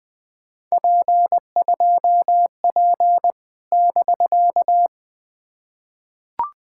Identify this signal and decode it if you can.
Morse code, heard as a keyed tone, about 20 wpm, 700 Hz. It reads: P2P <BK>
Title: POTA QSO Practice - 20wpm